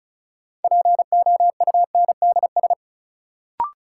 Morse code, heard as a keyed tone, about 36 wpm, 700 Hz. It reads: POUNDS
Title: Sentences from Top 1000 Words - Review Part 2 - New Word Drill - 35wpm